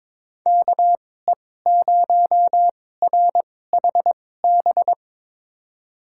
Morse code, heard as a keyed tone, about 22 wpm, spaced as for 16 wpm, 700 Hz. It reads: KE0RHB
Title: USA Call Signs - 2x Character Spacing - 22wpm